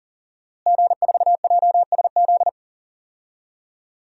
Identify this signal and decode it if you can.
Morse code, heard as a keyed tone, about 40 wpm, 700 Hz. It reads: G4JSZ
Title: DXCC-100 Call Signs - 5 Characters - Repeated 3x - 40wpm